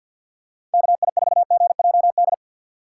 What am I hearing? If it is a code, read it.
Morse code, heard as a keyed tone, about 49 wpm, 700 Hz. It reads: KI4GJD
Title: USA Call Signs - 6 Characters - 3x Character Spacing - 50wpm